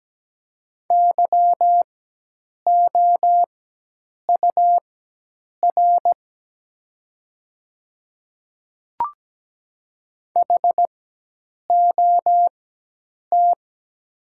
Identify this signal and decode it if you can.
Morse code, heard as a keyed tone, about 17 wpm, spaced as for 8 wpm, 690 Hz. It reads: YOUR HOT
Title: 4x Character Spacing - Top 100 Words - 17wpm